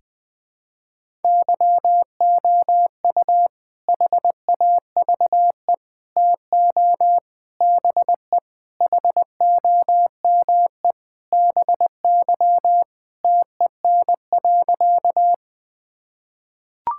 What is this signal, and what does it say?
Morse code, heard as a keyed tone, about 21 wpm, 700 Hz. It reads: YOU HAVE TO BE HOME BY TEN.
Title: Sentences with 4-Word Phrases - Spoken Beginning - 20wpm